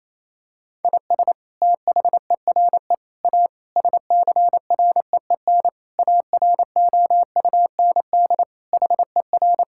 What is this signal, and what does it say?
Morse code, heard as a keyed tone, about 28 wpm, 700 Hz. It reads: IS THERE A SCREEN AROUND HER
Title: Sentences from Top 1000 Words - Review Part 6 - 28wpm